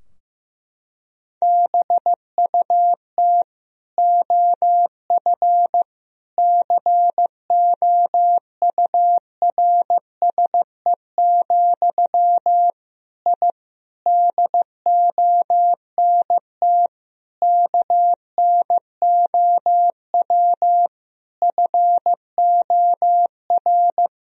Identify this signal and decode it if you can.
Morse code, heard as a keyed tone, about 16 wpm, 700 Hz. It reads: BUT OF COURSE, I DONT KNOW FOR
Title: Sentences from Top 1000 Words - Review Part 2 - 15wpm